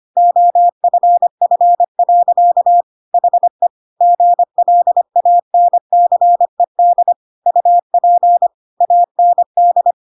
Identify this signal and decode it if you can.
Morse code, heard as a keyed tone, about 25 wpm, 700 Hz. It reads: OFF. HE GLANCED UP AND
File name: Pinocchio / Pinocchio ch3 25 25 ik1hge